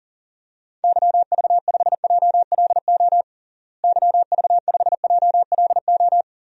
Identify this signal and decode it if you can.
Morse code, heard as a keyed tone, about 40 wpm, 700 Hz. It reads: YV5JLO YV5JLO
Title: DXCC-100 Call Signs - 6 Characters - Repeated 3x - 40wpm